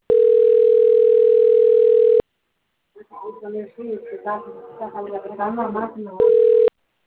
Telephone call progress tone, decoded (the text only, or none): ringback tone